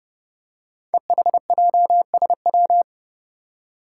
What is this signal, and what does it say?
Morse code, heard as a keyed tone, about 30 wpm, 700 Hz. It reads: EHJSW
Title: Code Groups - 5 Letters - 30wpm